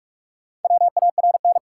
Morse code, heard as a keyed tone, about 44 wpm, 700 Hz. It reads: WARN
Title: Four Letter Words - 45wpm